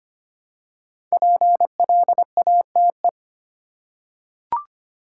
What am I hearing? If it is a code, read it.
Morse code, heard as a keyed tone, about 25 wpm, 700 Hz. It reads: PLATE
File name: 5 Letter Words - First 3 Letters Given - 25wpm